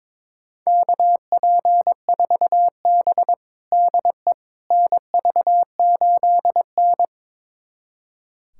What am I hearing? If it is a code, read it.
Morse code, heard as a keyed tone, about 22 wpm, 700 Hz. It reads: KP4B DE NV8N
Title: USA Callsign DE Callsign - 4 Characters - 22wpm